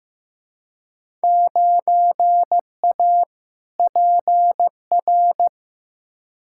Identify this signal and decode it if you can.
Morse code, heard as a keyed tone, about 15 wpm, 700 Hz. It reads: 9A PR
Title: Field Day Exchanges Encore - 15wpm